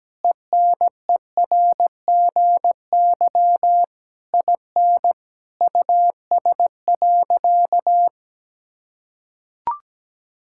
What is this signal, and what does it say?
Morse code, heard as a keyed tone, about 17 wpm, 700 Hz. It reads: ENERGY IN US.
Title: Sentences from Top 1000 Words - Review Part 6 - 17wpm